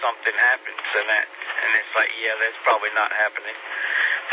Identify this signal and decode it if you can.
Emergency Alert System (SAME) tones, heard as harsh buzzing bursts, with speech. It is heard from 0.6 to 3.5 s.